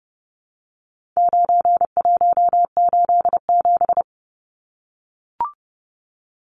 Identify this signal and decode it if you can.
Morse code, heard as a keyed tone, about 30 wpm, 700 Hz. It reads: 9187
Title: Numbers - 4-Digit 30wpm